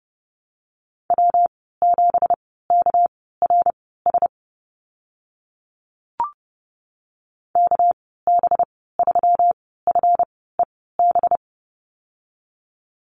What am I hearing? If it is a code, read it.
Morse code, heard as a keyed tone, about 32 wpm, spaced as for 17 wpm, 700 Hz. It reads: W7KRS KB3FEB
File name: Lesson 37g - Letter x - Call Signs 30wpm